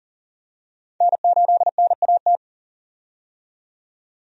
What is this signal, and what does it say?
Morse code, heard as a keyed tone, about 40 wpm, 700 Hz. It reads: N8NAT
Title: USA Call Signs - 5 Characters - 40wpm